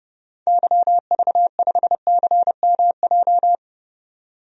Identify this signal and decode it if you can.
Morse code, heard as a keyed tone, about 30 wpm, 700 Hz. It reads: YV5CMJ
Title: DXCC-100 Call Signs - 6 Characters - 30wpm